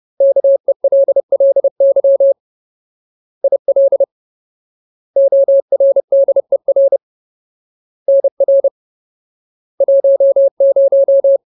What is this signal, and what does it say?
Morse code, heard as a keyed tone, about 30 wpm, 550 Hz. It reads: KELLY IL ORDER NR 10
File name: Ultimate Word List / 30WPM Ultimate